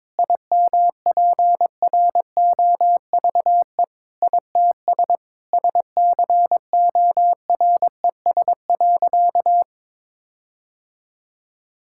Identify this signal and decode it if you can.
Morse code, heard as a keyed tone, about 22 wpm, 700 Hz. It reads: IMPROVE ITS SCORES.
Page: morse-code-ninja-practice-22wpm